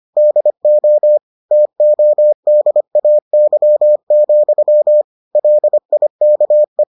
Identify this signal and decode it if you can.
Morse code, heard as a keyed tone, about 25 wpm, 600 Hz. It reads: DO TODAY, LIKE